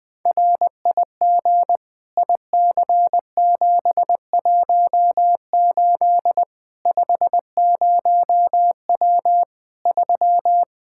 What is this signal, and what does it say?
Morse code, heard as a keyed tone, about 20 wpm, 700 Hz. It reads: RIG IC718 50W 3
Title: Practice QSO - Exchange 2 - Info - 20wpm